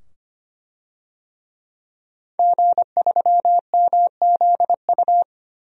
Morse code, heard as a keyed tone, about 25 wpm, 700 Hz. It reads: G3MZU
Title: DXCC-100 Call Signs - 5 Characters - 25wpm